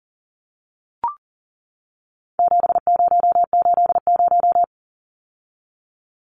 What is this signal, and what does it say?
Morse code, heard as a keyed tone, about 40 wpm, 700 Hz. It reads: 7080
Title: Numbers - 4-Digit 40wpm